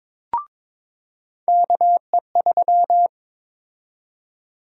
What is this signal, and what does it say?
Morse code, heard as a keyed tone, about 21 wpm, 700 Hz. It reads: KE3
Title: USA Call Sign Prefixes - 22wpm